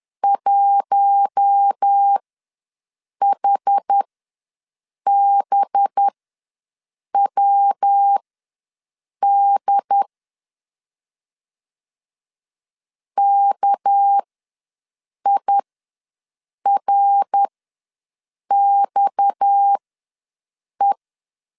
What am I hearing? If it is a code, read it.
Morse code, heard as a keyed tone, about 10 wpm, spaced as for 6 wpm, 800 Hz. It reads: 1HBWD KIRXE
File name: LEZIONI / 8 YJ45 / 8 Y ws 10wpm 3952